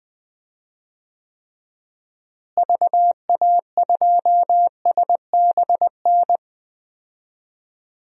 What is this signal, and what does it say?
Morse code, heard as a keyed tone, about 20 wpm, 690 Hz. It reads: VA2SBN
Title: DXCC-100 Call Signs - 6 Characters - Repeated 2x - 20wpm